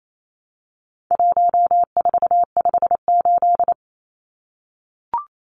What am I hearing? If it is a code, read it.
Morse code, heard as a keyed tone, about 28 wpm, 700 Hz. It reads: 1458
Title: Numbers - 4-Digit 28wpm